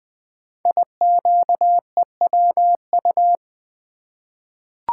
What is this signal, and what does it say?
Morse code, heard as a keyed tone, about 21 wpm, 700 Hz. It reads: IQEWU
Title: Code Groups - 5 Letters - 20wpm